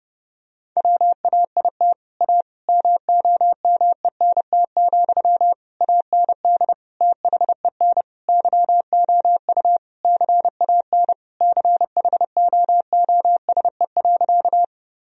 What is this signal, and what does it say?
Morse code, heard as a keyed tone, about 32 wpm, 700 Hz. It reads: WAIT A MOMENT, AND THEN YOU CAN CHOOSE.
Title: Sentences from Top 1000 Words - Review Part 4 - 30wpm